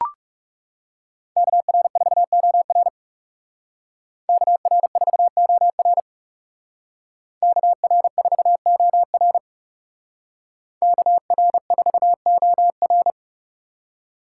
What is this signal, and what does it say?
Morse code, heard as a keyed tone, about 40 wpm, 700 Hz. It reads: KR4OR KR4OR KR4OR KR4OR